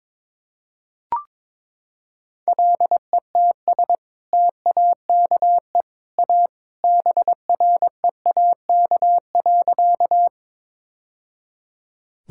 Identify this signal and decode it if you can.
Morse code, heard as a keyed tone, about 22 wpm, 700 Hz. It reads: LETS TAKE A BREAK.
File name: practice / 22wpm / Sentences with 2-Word Phrases - 22wpm